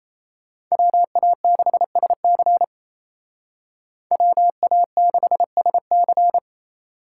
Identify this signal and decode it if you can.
Morse code, heard as a keyed tone, about 33 wpm, 700 Hz. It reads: WA6SC WA6SC
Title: Speed Racing - USA 2x2 Call Signs 28wpm